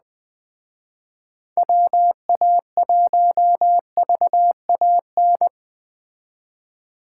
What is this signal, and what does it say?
Morse code, heard as a keyed tone, about 20 wpm, 700 Hz. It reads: WA1VAN